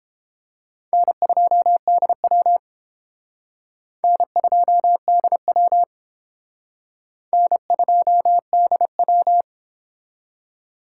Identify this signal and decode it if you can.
Morse code, heard as a keyed tone, about 30 wpm, 700 Hz. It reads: N2DW N2DW N2DW